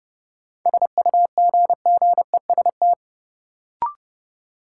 Morse code, heard as a keyed tone, about 30 wpm, 700 Hz. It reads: SUGGEST
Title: Sentences from Top 1300 Words - New Word Drill - 30wpm